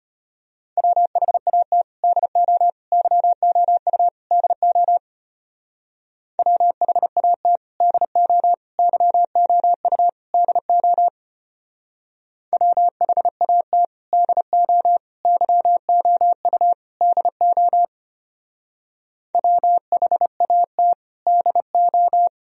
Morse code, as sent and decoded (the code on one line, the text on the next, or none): .-- .... .- - / -.. --- / -.-- --- ..- / -.. --- / .-- .... .- - / -.. --- / -.-- --- ..- / -.. --- / .-- .... .- - / -.. --- / -.-- --- ..- / -.. --- / .-- .... .- - / -.. ---
WHAT DO YOU DO WHAT DO YOU DO WHAT DO YOU DO WHAT DO